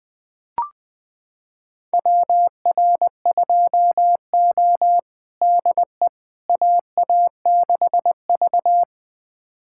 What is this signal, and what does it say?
Morse code, heard as a keyed tone, about 20 wpm, 700 Hz. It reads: WR2O DE AA6V
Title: USA Callsign DE Callsign - 4 Characters - 20wpm